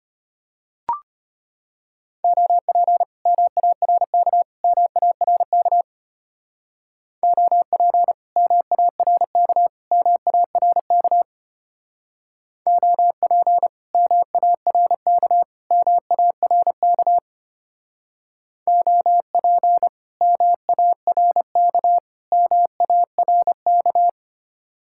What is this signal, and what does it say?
Morse code, heard as a keyed tone, about 32 wpm, 700 Hz. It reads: OP MARK MARK OP MARK MARK OP MARK MARK OP MARK MARK